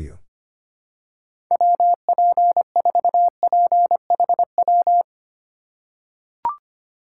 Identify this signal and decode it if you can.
Morse code, heard as a keyed tone, about 25 wpm, 700 Hz. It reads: WP4PHW
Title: USA Call Signs - 6 Characters - 4x Character Spacing - 25wpm